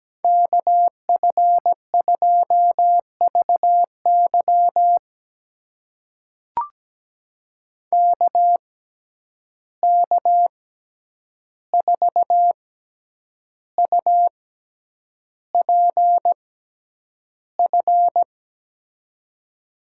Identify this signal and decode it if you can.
Morse code, heard as a keyed tone, about 17 wpm, spaced as for 8 wpm, 700 Hz. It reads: KF2VY KK4UPF
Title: USA Call Signs - 6x Character Spacing - 17wpm